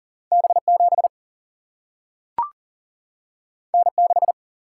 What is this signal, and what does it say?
Morse code, heard as a keyed tone, about 40 wpm, 700 Hz. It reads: B7 N6
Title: Single-Letter Single-Number 40wpm